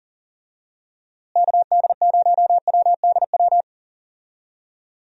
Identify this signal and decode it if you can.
Morse code, heard as a keyed tone, about 40 wpm, 700 Hz. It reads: KD0WDW
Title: USA Call Signs - 6 Characters - Repeated 3x - 40wpm